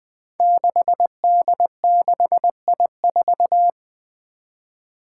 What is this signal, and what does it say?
Morse code, heard as a keyed tone, about 20 wpm, 700 Hz. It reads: 6D6I4
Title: Five Challenging Letter-Number Combinations 20wpm